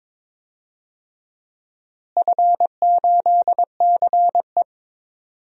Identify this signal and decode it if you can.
Morse code, heard as a keyed tone, about 22 wpm, 700 Hz. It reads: F8CE
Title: DXCC-100 Call Signs - 4 Characters - Repeated 3x - 22wpm